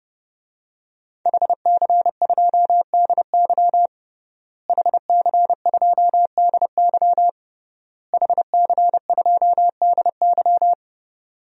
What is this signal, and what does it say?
Morse code, heard as a keyed tone, about 30 wpm, 700 Hz. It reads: HC2DY HC2DY HC2DY
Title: DXCC-100 Call Signs - 5 Characters - Repeated 3x - 30wpm